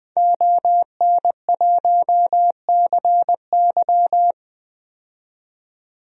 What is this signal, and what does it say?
Morse code, heard as a keyed tone, about 20 wpm, 700 Hz. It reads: ON1CY
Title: DXCC-100 Call Signs - 5 Characters - 20wpm